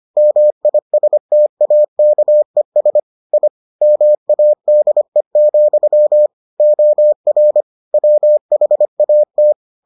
Morse code, heard as a keyed tone, about 25 wpm, 600 Hz. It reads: MISTAKES I MADE, OR WHAT